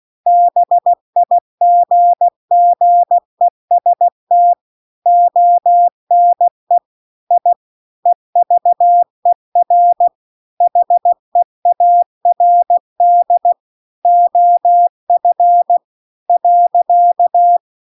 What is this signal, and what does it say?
Morse code, heard as a keyed tone, about 16 wpm, 700 Hz. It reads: BIGGEST ONE I EVER HEARD OF .
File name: CW-The-Big-Twoheaded-River-2-16WPM